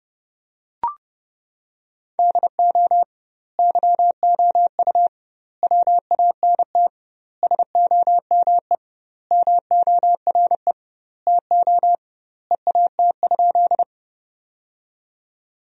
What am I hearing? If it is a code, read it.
Morse code, heard as a keyed tone, about 30 wpm, 700 Hz. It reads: DO YOU WANT SOME MORE TO EAT?
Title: 2x Word-Spacing - Sentences from Top 300 Words - 30wpm